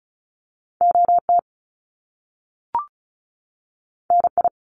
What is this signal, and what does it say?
Morse code, heard as a keyed tone, about 36 wpm, 700 Hz. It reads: O T N I
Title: Top Two Letter Combinations in Words 35wpm